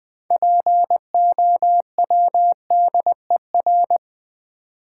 Morse code, heard as a keyed tone, about 20 wpm, 690 Hz. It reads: POWDER
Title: 6 Letter Words - First 3 Letters Given - 20wpm